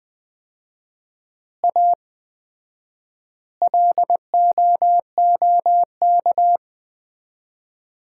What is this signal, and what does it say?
Morse code, heard as a keyed tone, about 20 wpm, 700 Hz. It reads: A LOOK
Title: 4x Word-Spacing - Sets of 5 Words - 20wpm